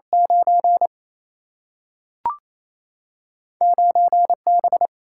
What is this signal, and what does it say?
Morse code, heard as a keyed tone, about 28 wpm, 700 Hz. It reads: 9 9B
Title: Single-Letter Single-Number 28wpm